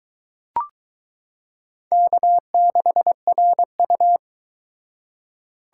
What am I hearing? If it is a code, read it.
Morse code, heard as a keyed tone, about 22 wpm, 700 Hz. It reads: K6RU